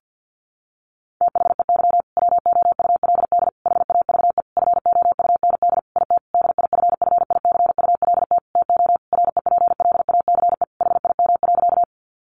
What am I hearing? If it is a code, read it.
Morse code, heard as a keyed tone, about 50 wpm, 700 Hz. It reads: THEY WOULD HAVE FOUND IT DIFFICULT TO REPLACE HIM.
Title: Sentences from Top 1400 Words 50wpm